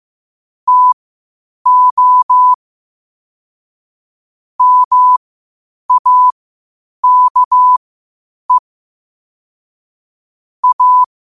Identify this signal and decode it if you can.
Morse code, heard as a keyed tone, about 15 wpm, spaced as for 8 wpm, 1000 Hz. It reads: TO MAKE A
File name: qotdfiles / 8wpm